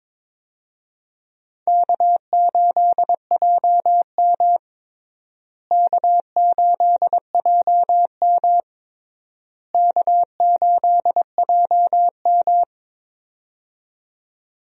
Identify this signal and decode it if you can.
Morse code, heard as a keyed tone, about 22 wpm, 700 Hz. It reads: K8JM K8JM K8JM
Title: USA Call Signs - 4 Characters - Repeated 3x - 22wpm